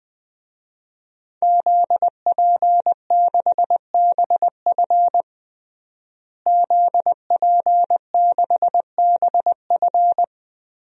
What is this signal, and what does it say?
Morse code, heard as a keyed tone, about 20 wpm, 700 Hz. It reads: ZP6BF ZP6BF
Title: DXCC-100 Call Signs - 5 Characters - Repeated 3x - 20wpm